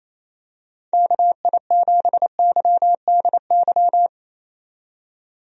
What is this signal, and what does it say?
Morse code, heard as a keyed tone, about 28 wpm, 700 Hz. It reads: KI7YDY
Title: USA Call Signs - 6 Characters - 28wpm